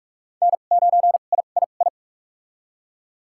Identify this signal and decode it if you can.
Morse code, heard as a keyed tone, about 46 wpm, spaced as for 33 wpm, 700 Hz. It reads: N9III
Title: USA Call Signs - 2x Character Spacing - 45wpm